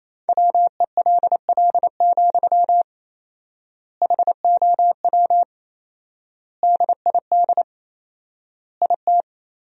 Morse code, heard as a keyed tone, about 28 wpm, 700 Hz. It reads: WELL, HOW DID IT
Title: 4x Word-Spacing - Sentences from Top 200 Words - 28wpm